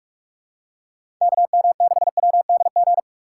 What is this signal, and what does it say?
Morse code, heard as a keyed tone, about 44 wpm, 700 Hz. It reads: KM6WDG
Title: USA Call Signs - 6 Characters - 4x Character Spacing - 45wpm